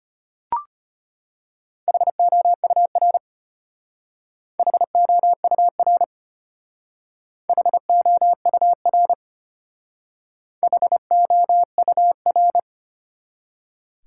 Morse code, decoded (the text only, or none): HOUR HOUR HOUR HOUR